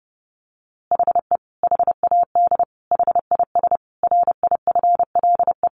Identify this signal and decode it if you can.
Morse code, heard as a keyed tone, about 30 wpm, 700 Hz. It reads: HE HAD HIS RIFLE
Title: A Sound Of Thunder 30wpm